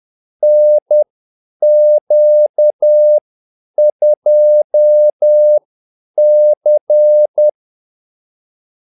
Morse code, heard as a keyed tone, about 10 wpm, spaced as for 8 wpm, 600 Hz. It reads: NQ2C
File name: Callsigns-10-8wpm